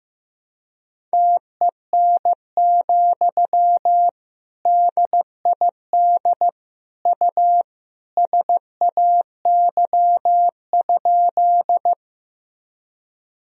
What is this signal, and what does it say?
Morse code, heard as a keyed tone, about 15 wpm, 700 Hz. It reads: TEN, DID U SAY?
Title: Sentences from Top 400 Words and Abbreviations 15wpm